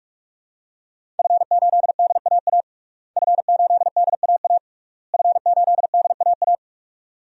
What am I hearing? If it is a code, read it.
Morse code, heard as a keyed tone, about 46 wpm, 700 Hz. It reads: F8DAA F8DAA F8DAA
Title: DXCC-100 Call Signs - 5 Characters - Repeated 3x - 45wpm